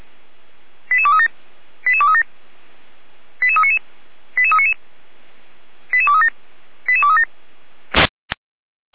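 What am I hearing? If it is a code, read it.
Five-tone selective call (ZVEI), heard as quick runs of five tones, tones of about 70 ms. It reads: 88237 88237 88399 88399 88237 88237
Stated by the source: Five-tone ZVEI selective call received by SDR.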